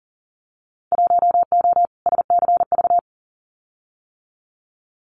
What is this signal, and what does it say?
Morse code, heard as a keyed tone, about 42 wpm, 700 Hz. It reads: 1O SCV
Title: Winter Field Day Exchanges 40wpm